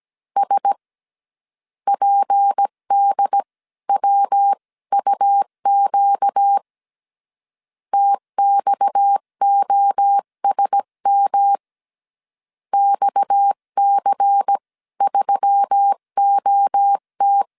Morse code, heard as a keyed tone, about 17 wpm, spaced as for 15 wpm, 800 Hz. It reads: S PDWUQ TXOSM XC3OT
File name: CW / DET / 1 6 1500C S C 16wpm 2733